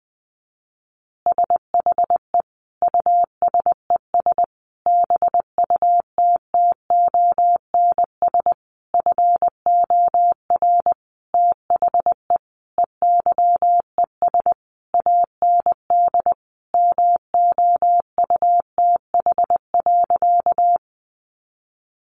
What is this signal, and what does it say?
Morse code, heard as a keyed tone, about 20 wpm, 700 Hz. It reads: SHE USES BUTTONS FOR THE EYES AND MOUTH.
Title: Easy Phrases 20wpm